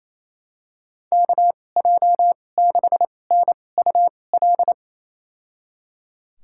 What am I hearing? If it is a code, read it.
Morse code, heard as a keyed tone, about 28 wpm, spaced as for 20 wpm, 700 Hz. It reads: KJ6NUL
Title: USA Call Signs - 6 Characters - 2x Character Spacing - 28wpm